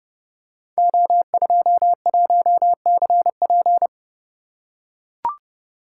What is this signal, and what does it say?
Morse code, heard as a keyed tone, about 29 wpm, 700 Hz. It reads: O21CP
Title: Code Groups - 5 Letter:Numbers - 30wpm